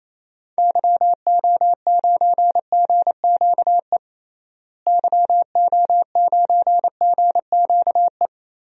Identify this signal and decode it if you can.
Morse code, heard as a keyed tone, about 28 wpm, 700 Hz. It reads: YO9GQE YO9GQE
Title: DXCC-100 Call Signs - 6 Characters - Repeated 2x - 28wpm